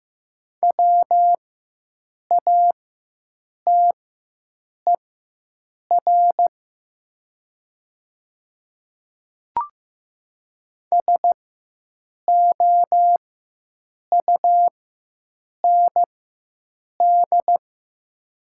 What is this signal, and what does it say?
Morse code, heard as a keyed tone, about 15 wpm, spaced as for 7 wpm, 700 Hz. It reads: WATER SOUND
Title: 4x Character Spacing - Top 200 Words - 15wpm